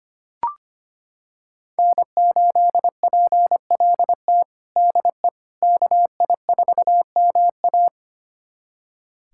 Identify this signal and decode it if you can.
Morse code, heard as a keyed tone, about 25 wpm, 700 Hz. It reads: N8PLT DE KI4MA
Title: USA Callsign DE Callsign - 5 Characters - 25wpm